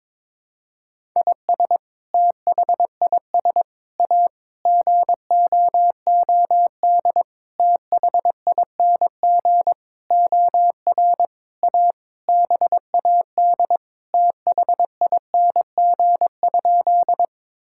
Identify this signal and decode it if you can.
Morse code, heard as a keyed tone, about 22 wpm, 700 Hz. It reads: IS THIS A GOOD THING OR A BAD THING?
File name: Sentences with 7-Word Phrases - 22wpm